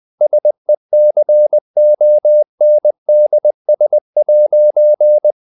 Morse code, heard as a keyed tone, about 20 wpm, 600 Hz. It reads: SECONDS'
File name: CW / Practice / If20